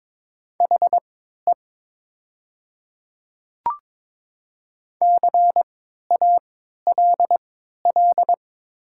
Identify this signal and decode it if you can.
Morse code, heard as a keyed tone, about 22 wpm, spaced as for 12 wpm, 700 Hz. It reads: HE CALL
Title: 3x Character Spacing - Top 100 Words - 22wpm